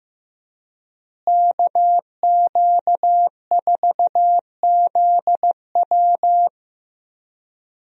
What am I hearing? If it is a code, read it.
Morse code, heard as a keyed tone, about 15 wpm, 700 Hz. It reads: KQ4ZW
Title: USA Call Signs - 5 Characters - Repeated 2x - 15wpm